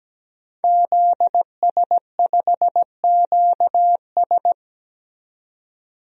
Morse code, heard as a keyed tone, about 17 wpm, 700 Hz. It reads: ZS5QS
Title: DXCC-100 Call Signs - 5 Characters - 17wpm